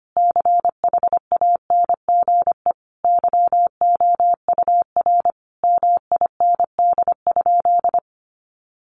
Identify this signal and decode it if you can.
Morse code, heard as a keyed tone, about 25 wpm, 700 Hz. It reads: CHANGE YOUR MIND?
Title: Sentences from Top 1300 Words 25wpm